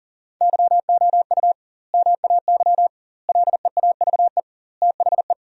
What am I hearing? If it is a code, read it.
Morse code, heard as a keyed tone, about 40 wpm, 700 Hz. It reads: YOU MAY LEAVE THE
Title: 2x Word-Spacing - Sentences from Top 400 Words - 40wpm